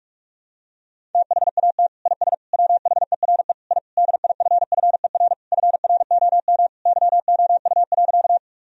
Morse code, heard as a keyed tone, about 44 wpm, 700 Hz. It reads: THAT IS WHERE I DIFFER FROM YOU.
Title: Sentences from Top 1000 Words - Review Part 1 - 45wpm